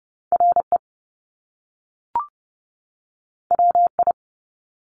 Morse code, heard as a keyed tone, about 30 wpm, 700 Hz. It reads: RE WI
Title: Top Two Letter Combinations in Words 30wpm